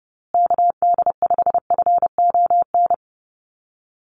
Morse code, heard as a keyed tone, about 30 wpm, 700 Hz. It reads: KD5FON
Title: USA Call Signs Repeated 3x 30wpm